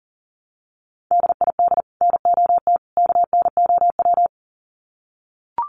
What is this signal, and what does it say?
Morse code, heard as a keyed tone, about 40 wpm, 700 Hz. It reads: DID NOT KNOW
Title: Sets of 3 Words 40wpm